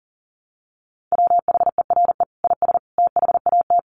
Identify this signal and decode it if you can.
Morse code, heard as a keyed tone, about 40 wpm, 700 Hz. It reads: WHERE IS THAT
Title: Sentences from Top 1300 Words 40wpm